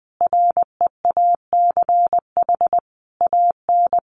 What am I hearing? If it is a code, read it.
Morse code, heard as a keyed tone, about 20 wpm, 700 Hz. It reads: REACH AN
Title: Sentences from Top 1500 Words 20wpm